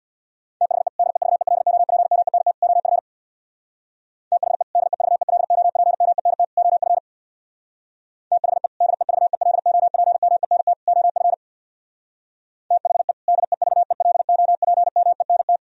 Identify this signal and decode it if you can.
Morse code, heard as a keyed tone, about 64 wpm, 700 Hz. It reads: THE DEVELOPMENT OF THE DEVELOPMENT OF THE DEVELOPMENT OF THE DEVELOPMENT